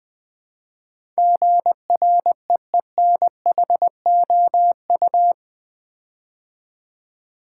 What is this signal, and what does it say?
Morse code, heard as a keyed tone, about 20 wpm, 700 Hz. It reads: GREENHOU
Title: Build A Word - 4 Letters At A Time - 10-Letter Words - 20wpm